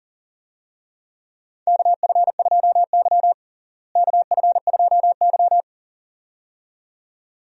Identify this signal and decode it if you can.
Morse code, heard as a keyed tone, about 37 wpm, 700 Hz. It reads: KF2Y KF2Y
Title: USA Call Signs - 4 Characters - Repeated 2x - 40wpm